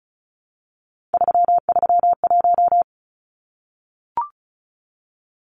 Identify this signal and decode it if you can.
Morse code, heard as a keyed tone, about 34 wpm, 700 Hz. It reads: 331
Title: Numbers - 3-Digit 35wpm